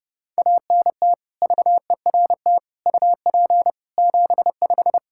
Morse code, heard as a keyed tone, about 30 wpm, 700 Hz. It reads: ANT VERT UP 75
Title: K1USN SST QSO Practice 30wpm